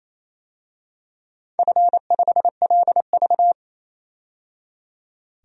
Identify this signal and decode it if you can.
Morse code, heard as a keyed tone, about 28 wpm, 700 Hz. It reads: F5LV
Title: DXCC-100 Call Signs - 4 Characters - 28wpm